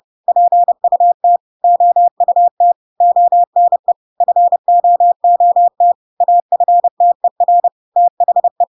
Morse code, heard as a keyed tone, about 30 wpm, 700 Hz. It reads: PUT OUT ONE FOOT AFTER THE
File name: Pinocchio / Pinocchio ch3 30 30 ik1hge